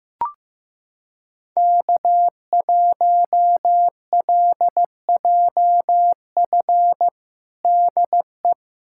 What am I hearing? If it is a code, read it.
Morse code, heard as a keyed tone, about 15 wpm, 700 Hz. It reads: K1LJF DE
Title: USA Callsign DE Callsign 15wpm